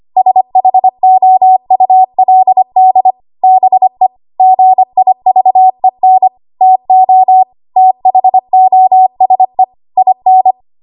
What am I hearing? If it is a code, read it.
Morse code, heard as a keyed tone, about 25 wpm, 750 Hz. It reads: SHOULD BE GIVEN TO THOSE IN
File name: Morse / 25WPM